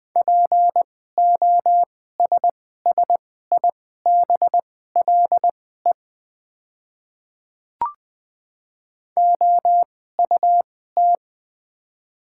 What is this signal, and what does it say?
Morse code, heard as a keyed tone, about 20 wpm, spaced as for 14 wpm, 690 Hz. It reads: POSSIBLE OUT
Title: 2x Character Spacing - Top 500 Words - 20wpm